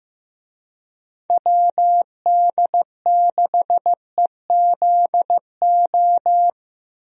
Morse code, heard as a keyed tone, about 15 wpm, 700 Hz. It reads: WD6EZO
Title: USA Call Signs - 6 Characters - Repeated 3x - 15wpm